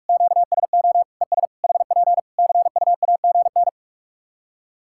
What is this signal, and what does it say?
Morse code, heard as a keyed tone, about 46 wpm, 700 Hz. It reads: QSO ES HP CUAGN
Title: K1USN SST QSO Practice 45wpm